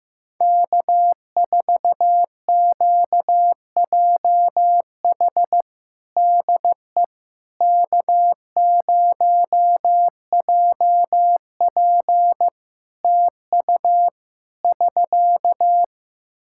Morse code, heard as a keyed tone, about 15 wpm, 700 Hz. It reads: K4QJH DE K0JP TU <SK>
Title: Practice QSO - Exchange 4 - Ending - 15wpm